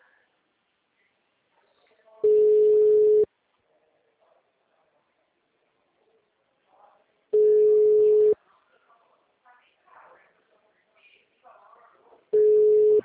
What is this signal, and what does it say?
Ringback tone (Europe, 425 Hz), with speech.